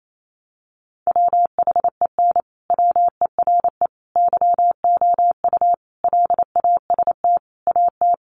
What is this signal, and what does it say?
Morse code, heard as a keyed tone, about 28 wpm, 700 Hz. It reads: WHEN WERE YOU LAST AT
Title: Sentences from Top 1300 Words 28wpm